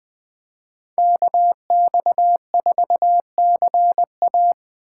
Morse code, heard as a keyed tone, about 21 wpm, 700 Hz. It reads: KX4CA
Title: Speed Racing - USA 2x2 Call Signs 17wpm